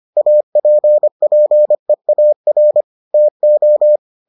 Morse code, heard as a keyed tone, about 25 wpm, 600 Hz. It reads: APPEAR TO